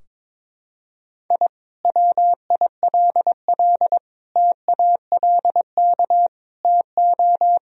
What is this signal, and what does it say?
Morse code, heard as a keyed tone, about 22 wpm, 700 Hz. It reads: I WILL TALK TO
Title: Sentences from Top 1000 Words - Review Part 2 - 22wpm